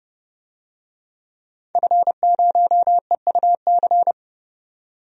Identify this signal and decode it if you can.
Morse code, heard as a keyed tone, about 30 wpm, 700 Hz. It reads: F0EUC
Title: DXCC-100 Call Signs - 5 Characters - 30wpm